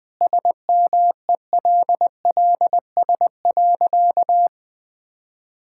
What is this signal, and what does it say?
Morse code, heard as a keyed tone, about 21 wpm, 700 Hz. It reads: SMELLS.
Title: All-Age Appropriate Jokes - 20wpm